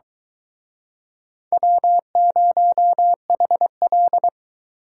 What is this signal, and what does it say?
Morse code, heard as a keyed tone, about 23 wpm, 700 Hz. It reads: W0HL